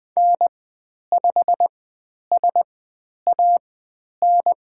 Morse code, heard as a keyed tone, about 20 wpm, spaced as for 10 wpm, 700 Hz. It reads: N5SAN
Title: 102 Lesson 17g - Letter u - Call Signs 20wpm